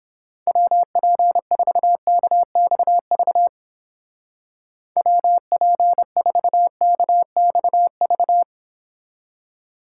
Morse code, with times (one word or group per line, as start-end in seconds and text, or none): 0.5-3.5 WP4KXV
5.0-8.4 WP4KXV